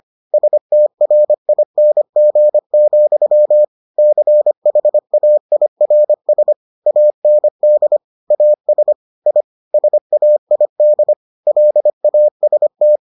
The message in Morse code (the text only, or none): STRING, CHAIRS AND AS I SAID LAST